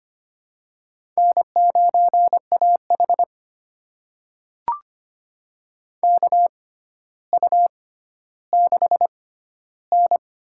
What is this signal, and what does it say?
Morse code, heard as a keyed tone, about 25 wpm, spaced as for 13 wpm, 700 Hz. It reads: N9AH KU6N
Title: USA Call Signs - 4 Characters - 6x Character Spacing - 25wpm